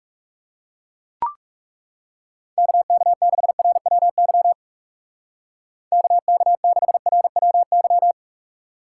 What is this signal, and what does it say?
Morse code, heard as a keyed tone, about 37 wpm, 700 Hz. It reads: KK6RWY KK6RWY
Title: Speed Racing - USA Call Signs 30wpm